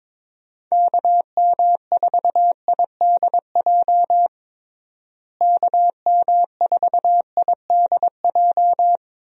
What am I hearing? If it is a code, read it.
Morse code, heard as a keyed tone, about 22 wpm, 700 Hz. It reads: KM4IDJ KM4IDJ
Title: USA Call Signs - 6 Characters - Repeated 3x - 22wpm